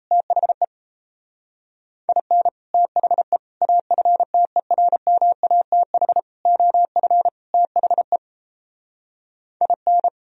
Morse code, heard as a keyed tone, about 38 wpm, 700 Hz. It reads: THE IN THE AFTERMATH OF THE IN